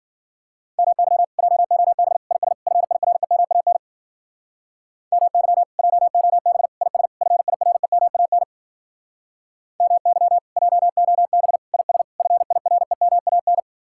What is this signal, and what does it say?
Morse code, heard as a keyed tone, about 51 wpm, 700 Hz. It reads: MY JOB IS FIREMAN MY JOB IS FIREMAN MY JOB IS FIREMAN